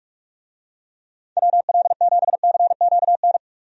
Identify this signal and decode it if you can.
Morse code, heard as a keyed tone, about 41 wpm, 700 Hz. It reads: WL7CQN
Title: USA Call Signs - 6 Characters - 45wpm